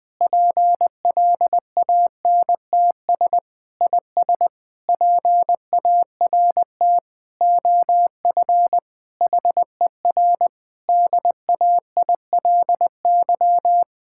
Morse code, heard as a keyed tone, about 21 wpm, 700 Hz. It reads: PLANTS IS PART OF HER DAILY
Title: Difficult Sentences with Hints - 20wpm